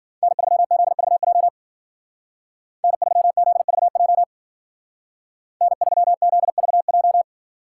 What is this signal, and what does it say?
Morse code, heard as a keyed tone, about 50 wpm, 700 Hz. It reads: N3ZVJ N3ZVJ N3ZVJ
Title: Speed Racing - USA 1x3 Call Signs 40wpm